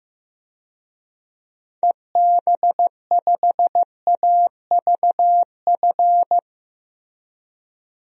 Morse code, heard as a keyed tone, about 15 wpm, 700 Hz. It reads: EB5AVF
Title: DXCC-100 Call Signs - 6 Characters - Repeated 2x - 15wpm